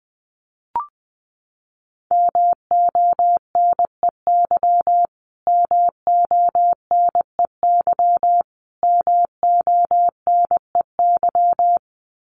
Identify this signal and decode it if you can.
Morse code, heard as a keyed tone, about 20 wpm, 700 Hz. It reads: MONEY MONEY MONEY
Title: Top 500 Words Repeated 20wpm